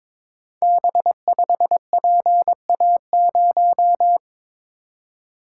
Morse code, heard as a keyed tone, about 22 wpm, 700 Hz. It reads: B5PA0
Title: Code Groups - 5 Letter:Numbers - 22wpm